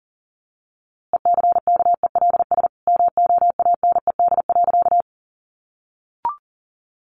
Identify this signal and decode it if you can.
Morse code, heard as a keyed tone, about 40 wpm, 700 Hz. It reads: ECKELS MOANED.
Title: A Sound Of Thunder 40wpm